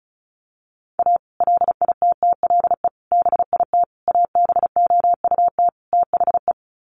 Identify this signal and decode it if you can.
Morse code, heard as a keyed tone, about 34 wpm, 700 Hz. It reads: A LITTLE BIT ABOUT THE
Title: Sets of 5 Words Encore2 35wpm